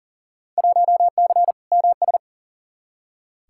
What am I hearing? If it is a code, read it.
Morse code, heard as a keyed tone, about 40 wpm, 700 Hz. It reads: 1C MS
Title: Field Day Exchanges Encore - 40wpm